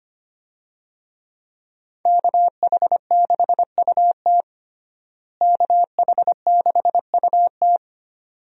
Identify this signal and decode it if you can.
Morse code, heard as a keyed tone, about 25 wpm, 700 Hz. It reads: KH6UT KH6UT
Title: USA Call Signs - 5 Characters - Repeated 2x - 25wpm